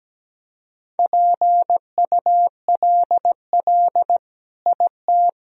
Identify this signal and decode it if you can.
Morse code, heard as a keyed tone, about 17 wpm, 700 Hz. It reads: PULL IT
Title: Sentences from Top 400 Words and Abbreviations 17wpm